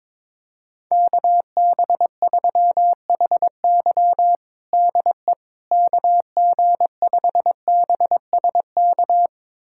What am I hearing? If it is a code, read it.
Morse code, heard as a keyed tone, about 22 wpm, 700 Hz. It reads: KB3HY DE KG5BSK
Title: USA Callsign DE Callsign 22wpm